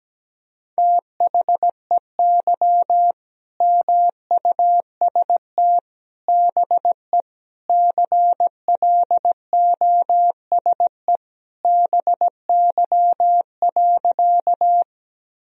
Morse code, heard as a keyed tone, about 17 wpm, 700 Hz. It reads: THEY MUST BE CLOSE BY.
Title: Sentences from Top 1000 Words - Review Part 1 - 17wpm